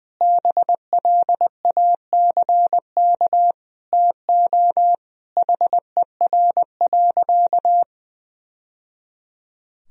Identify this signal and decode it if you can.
Morse code, heard as a keyed tone, about 19 wpm, 700 Hz. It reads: BLACK TO HER.
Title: Sentences from Top 1000 Words - Review Part 2 - 20wpm